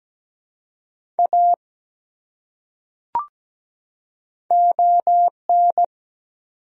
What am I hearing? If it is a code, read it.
Morse code, heard as a keyed tone, about 17 wpm, 700 Hz. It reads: A ON
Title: Top 100 Words 17wpm